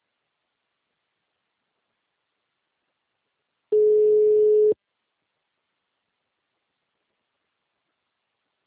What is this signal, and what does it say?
Ringback tone (Europe, 425 Hz).